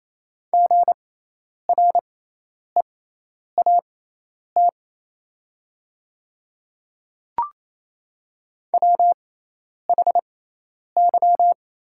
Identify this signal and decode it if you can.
Morse code, heard as a keyed tone, about 28 wpm, spaced as for 10 wpm, 690 Hz. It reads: GREAT WHY
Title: 6x Character Spacing - Top 200 Words - 28wpm